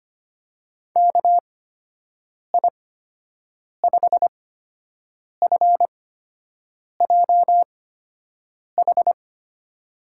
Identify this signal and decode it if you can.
Morse code, heard as a keyed tone, about 25 wpm, spaced as for 7 wpm, 700 Hz. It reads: KI5FJH
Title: USA Call Signs - 6 Characters - 8x Character Spacing - 25wpm